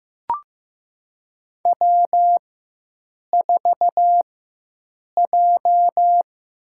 Morse code, heard as a keyed tone, about 15 wpm, spaced as for 7 wpm, 700 Hz. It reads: W4J
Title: USA Call Signs - 5 Characters - 4x Character Spacing - 15wpm